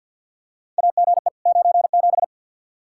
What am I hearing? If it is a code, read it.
Morse code, heard as a keyed tone, about 50 wpm, 700 Hz. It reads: AGE 97
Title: Practice QSO - Exchange 3 - More Info - 50wpm